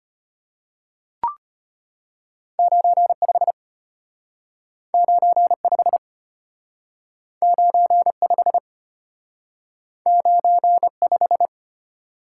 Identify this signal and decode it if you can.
Morse code, heard as a keyed tone, about 34 wpm, 700 Hz. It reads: 95 95 95 95